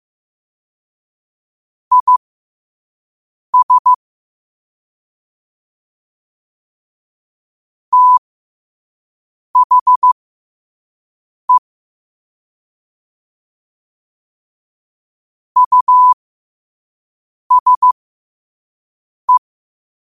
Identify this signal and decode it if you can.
Morse code, heard as a keyed tone, about 15 wpm, spaced as for 5 wpm, 1000 Hz. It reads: IS THE USE
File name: qotdfiles / 5wpm